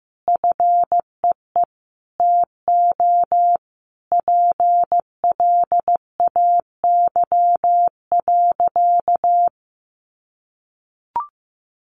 Morse code, heard as a keyed tone, about 15 wpm, 700 Hz. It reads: FEE TO PLAY.
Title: Sentences from Top 1500 Words 15wpm